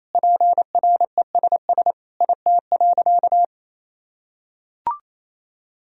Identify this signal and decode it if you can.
Morse code, heard as a keyed tone, about 28 wpm, 700 Hz. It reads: PRESS IT.
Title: Sentences from Top 1000 Words - Review Part 1 - 28wpm